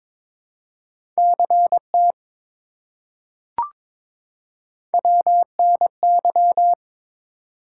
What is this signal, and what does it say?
Morse code, heard as a keyed tone, about 22 wpm, 700 Hz. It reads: CT WNY
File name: ARRL Sections v2 - 22wpm